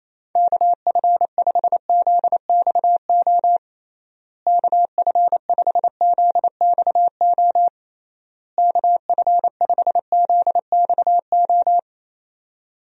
Morse code, heard as a keyed tone, about 28 wpm, 700 Hz. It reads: KF5ZXO KF5ZXO KF5ZXO
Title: USA Call Signs - 6 Characters - Repeated 3x - 28wpm